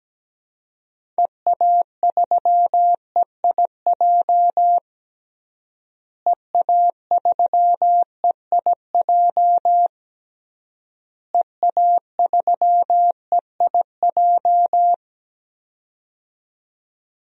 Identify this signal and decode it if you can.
Morse code, heard as a keyed tone, about 17 wpm, 700 Hz. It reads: EA3EIJ EA3EIJ EA3EIJ
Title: DXCC-100 Call Signs - 6 Characters - Repeated 3x - 17wpm